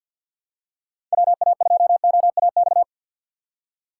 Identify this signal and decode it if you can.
Morse code, heard as a keyed tone, about 47 wpm, 700 Hz. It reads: WA2OAX
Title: USA Call Signs - 6 Characters - 4x Character Spacing - 50wpm